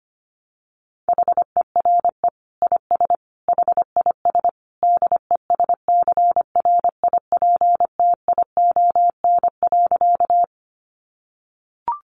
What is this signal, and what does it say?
Morse code, heard as a keyed tone, about 25 wpm, 700 Hz. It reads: HERE IS HIS DESCRIPTION.
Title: Sentences from Top 1300 Words 25wpm